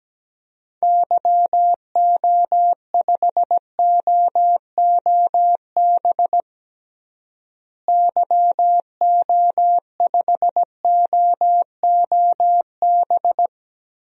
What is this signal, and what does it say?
Morse code, heard as a keyed tone, about 17 wpm, 700 Hz. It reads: YO5OOB YO5OOB
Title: DXCC-100 Call Signs - 6 Characters - Repeated 3x - 17wpm